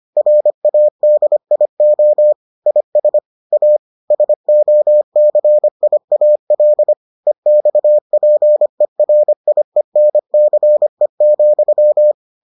Morse code, heard as a keyed tone, about 25 wpm, 600 Hz. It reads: RADIO IS A SOCIAL EXPERIENCE,